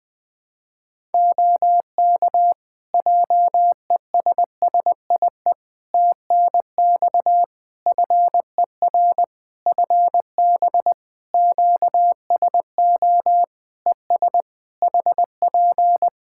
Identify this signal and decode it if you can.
Morse code, heard as a keyed tone, about 20 wpm, 700 Hz. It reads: OK JESSIE TNX FER FB QSO ES HP
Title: Practice QSO - Exchange 4 - Ending - 20wpm